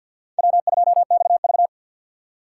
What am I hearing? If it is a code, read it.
Morse code, heard as a keyed tone, about 52 wpm, 700 Hz. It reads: W2XV
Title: USA Call Signs - 4 Characters - 6x Character Spacing - 50wpm